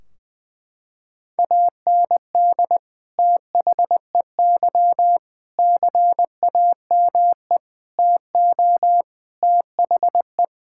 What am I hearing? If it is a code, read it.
Morse code, heard as a keyed tone, about 21 wpm, 700 Hz. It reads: AND THEY CAME TO THE
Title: Sentences from Top 1600 Words 20wpm